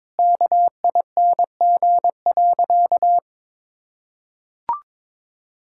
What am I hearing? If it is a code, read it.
Morse code, heard as a keyed tone, about 22 wpm, 700 Hz. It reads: KING.
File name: practice / 22wpm / Sentences with 4-Word Phrases - 22wpm